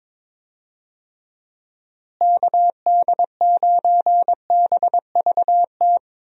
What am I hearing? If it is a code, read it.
Morse code, heard as a keyed tone, about 22 wpm, 700 Hz. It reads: KD9BVT
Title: USA Call Signs - 6 Characters - Repeated 2x - 22wpm